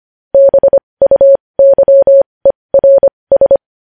Morse code, heard as a keyed tone, about 25 wpm, spaced as for 20 wpm, 550 Hz. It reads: BUYERS
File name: podcast / News headlines-1x-25x20